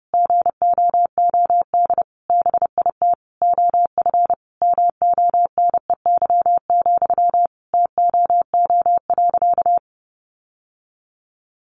Morse code, heard as a keyed tone, about 30 wpm, 700 Hz. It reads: GOOD BIT OF MONEY, TOO.
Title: Sentences from Top 900 Words 30wpm